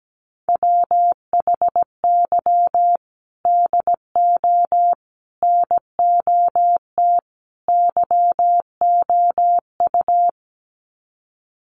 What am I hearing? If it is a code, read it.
Morse code, heard as a keyed tone, about 17 wpm, 700 Hz. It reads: WHY DO NOT YOU
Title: Sets of 4 Words 17wpm